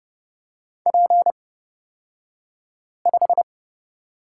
Morse code, heard as a keyed tone, about 30 wpm, spaced as for 5 wpm, 700 Hz. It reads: P5
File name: Lesson 36d - BT - Two Characters 30wpm